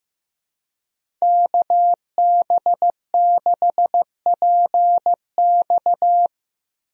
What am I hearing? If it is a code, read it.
Morse code, heard as a keyed tone, about 15 wpm, 700 Hz. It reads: KB6PX
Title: USA Call Signs - 5 Characters - Repeated 3x - 15wpm